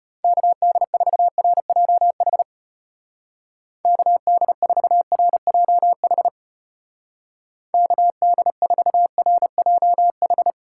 Morse code, heard as a keyed tone, about 32 wpm, 700 Hz. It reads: KD4RJH KD4RJH KD4RJH